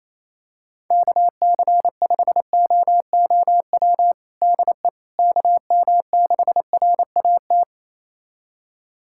Morse code, heard as a keyed tone, about 28 wpm, 700 Hz. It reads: KC5OOW DE KM6RAT
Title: USA Callsign DE Callsign - 6 Characters - 28wpm